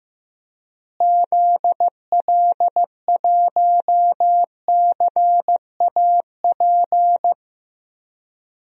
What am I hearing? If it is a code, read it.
Morse code, heard as a keyed tone, about 15 wpm, 700 Hz. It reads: ZL1CAP
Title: DXCC-100 Call Signs - 6 Characters - Repeated 2x - 15wpm